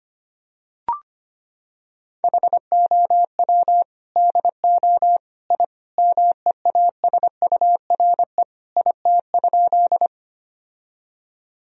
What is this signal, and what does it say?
Morse code, heard as a keyed tone, about 25 wpm, 700 Hz. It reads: HOW DO I MEASURE IT?
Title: Sentences from Top 1000 Words - Review Part 2 - 25wpm